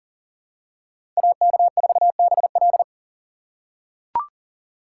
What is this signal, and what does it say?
Morse code, heard as a keyed tone, about 40 wpm, 700 Hz. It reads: AK4BL
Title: USA Call Signs - 5 Characters - 4x Character Spacing - 40wpm